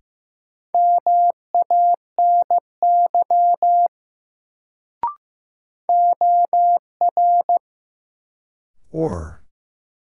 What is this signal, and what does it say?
Morse code, heard as a keyed tone, about 16 wpm, 700 Hz. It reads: MANY OR
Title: Sentences from Top 100 Words - New Word Drill - 15wpm